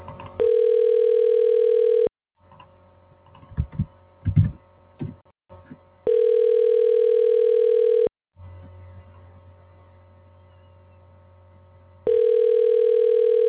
Ringback tone (North America), 440 + 480 Hz.